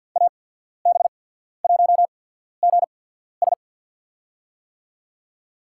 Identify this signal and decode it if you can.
Morse code, heard as a keyed tone, about 53 wpm, spaced as for 14 wpm, 700 Hz. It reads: AB1GS
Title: USA Call Signs - 5 Characters - 8x Character Spacing - 50wpm